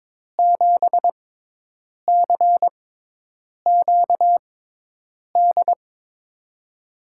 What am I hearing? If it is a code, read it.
Morse code, heard as a keyed tone, about 22 wpm, 700 Hz. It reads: 7 C Q D
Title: USA Call Signs - 6x Character Spacing - 22wpm